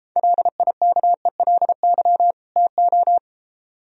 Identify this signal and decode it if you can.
Morse code, heard as a keyed tone, about 33 wpm, 700 Hz. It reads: LIKELY TO